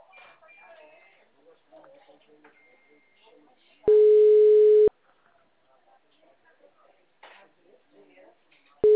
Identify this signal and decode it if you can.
Ringback tone (Europe, 425 Hz), with speech.